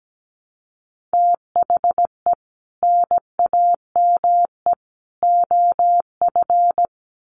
Morse code, heard as a keyed tone, about 17 wpm, 700 Hz. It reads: THE NAME OF
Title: Sets of 3 Words 17wpm